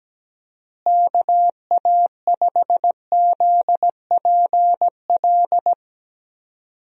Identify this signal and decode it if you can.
Morse code, heard as a keyed tone, about 17 wpm, 700 Hz. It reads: KA5ZPL
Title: USA Call Signs - 6 Characters - Repeated 2x - 17wpm